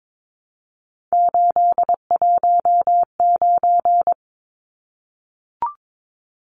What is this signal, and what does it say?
Morse code, heard as a keyed tone, about 22 wpm, 700 Hz. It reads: 819
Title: Numbers - 3-Digit 22wpm